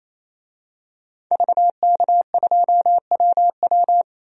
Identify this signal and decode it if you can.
Morse code, heard as a keyed tone, about 28 wpm, 700 Hz. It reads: VK2WW
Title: DXCC-100 Call Signs - 5 Characters - 28wpm